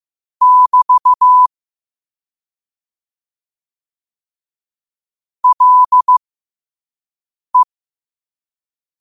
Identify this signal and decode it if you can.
Morse code, heard as a keyed tone, about 15 wpm, spaced as for 5 wpm, 1000 Hz. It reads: <BT> LE
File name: qotdfiles / 5wpm